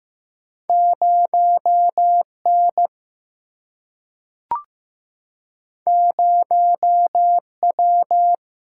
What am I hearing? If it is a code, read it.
Morse code, heard as a keyed tone, about 15 wpm, 700 Hz. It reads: 0 N 0 W
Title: Single-Letter Single-Number 15wpm